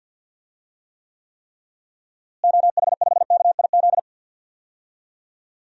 Morse code, heard as a keyed tone, about 51 wpm, 700 Hz. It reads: OH5KIZ
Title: DXCC-100 Call Signs - 6 Characters - Repeated 3x - 50wpm